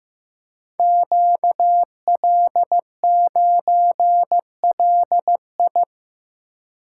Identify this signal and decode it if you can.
Morse code, heard as a keyed tone, about 15 wpm, 700 Hz. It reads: QL9LI
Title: Code Groups - 5 Letter:Numbers - 15wpm